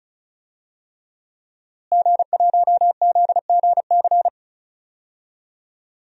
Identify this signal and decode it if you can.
Morse code, heard as a keyed tone, about 36 wpm, 700 Hz. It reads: G1ZGC
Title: DXCC-100 Call Signs - 5 Characters - Repeated 3x - 35wpm